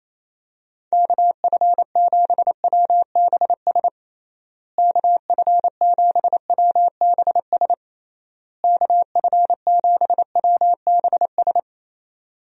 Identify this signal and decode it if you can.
Morse code, heard as a keyed tone, about 28 wpm, 700 Hz. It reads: KF7WBS KF7WBS KF7WBS
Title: USA Call Signs - 6 Characters - Repeated 3x - 28wpm